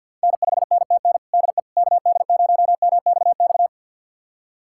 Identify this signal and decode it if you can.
Morse code, heard as a keyed tone, about 51 wpm, 700 Hz. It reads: N5NTN DE KD0MXX
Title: USA Callsign DE Callsign 50wpm